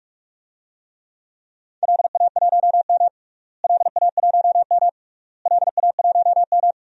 Morse code, heard as a keyed tone, about 46 wpm, 700 Hz. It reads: LA1M LA1M LA1M
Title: DXCC-100 Call Signs - 4 Characters - Repeated 3x - 45wpm